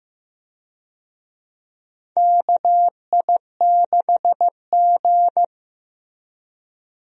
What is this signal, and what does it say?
Morse code, heard as a keyed tone, about 15 wpm, 700 Hz. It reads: KI6G
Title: USA Call Signs - 4 Characters - Repeated 3x - 15wpm